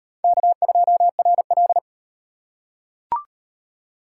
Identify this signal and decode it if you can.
Morse code, heard as a keyed tone, about 37 wpm, 700 Hz. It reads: K2RL